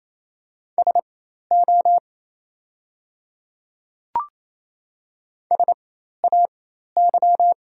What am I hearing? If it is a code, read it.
Morse code, heard as a keyed tone, about 28 wpm, spaced as for 13 wpm, 700 Hz. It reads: SO SAY
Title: 4x Character Spacing - Top 300 Words - 28wpm